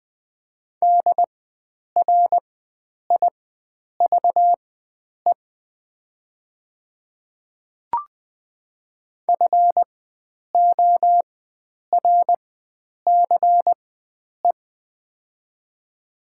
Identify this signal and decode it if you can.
Morse code, heard as a keyed tone, about 20 wpm, spaced as for 9 wpm, 700 Hz. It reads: DRIVE FORCE
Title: 4x Character Spacing - Top 500 Words - 20wpm